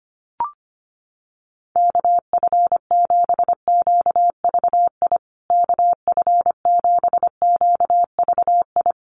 Morse code, heard as a keyed tone, about 25 wpm, 700 Hz. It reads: KF7QVI KF7QVI
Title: USA Call Signs Repeated 3x 25wpm